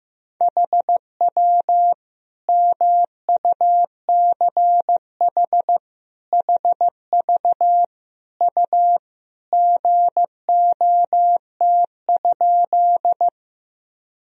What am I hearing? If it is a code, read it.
Morse code, heard as a keyed tone, about 15 wpm, 700 Hz. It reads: HW MUCH HV U GOT?
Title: Sentences from Top 300 Words and Abbreviations 15wpm